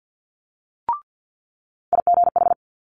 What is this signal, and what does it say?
Morse code, heard as a keyed tone, about 50 wpm, 700 Hz. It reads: IGH
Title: Multi-letter Phonemes Quick Fire 50wpm